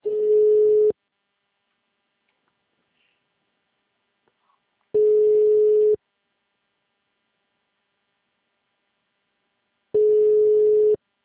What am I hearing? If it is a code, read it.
Ringback tone (Europe, 425 Hz).